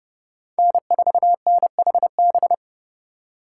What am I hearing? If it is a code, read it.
Morse code, heard as a keyed tone, about 30 wpm, 700 Hz. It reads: N4NHB
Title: USA Call Signs - 5 Characters - 3x Character Spacing - 30wpm